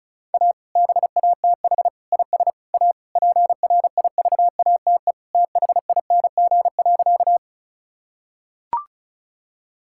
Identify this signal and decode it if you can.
Morse code, heard as a keyed tone, about 34 wpm, 700 Hz. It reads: A BATH IS A PRIVATE THING.
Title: Sentences from Top 1600 Words 35wpm